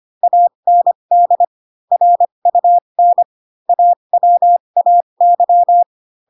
Morse code, heard as a keyed tone, about 25 wpm, spaced as for 22 wpm, 700 Hz. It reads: AND RUN AWAY
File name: Pinocchio / Pinocchio ch4 18 25 ik1hge